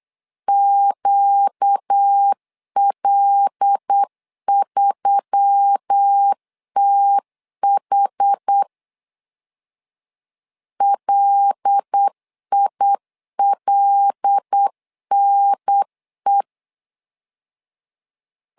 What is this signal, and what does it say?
Morse code, heard as a keyed tone, about 8 wpm, 800 Hz. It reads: QL3TH LILNE
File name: LEZIONI / 5 LBVC / 5 L 8wpm 2831